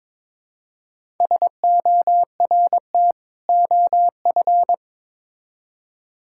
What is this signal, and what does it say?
Morse code, heard as a keyed tone, about 22 wpm, 700 Hz. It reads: SORT OF
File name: Build A Phrase - 2-Word Phrases - 22wpm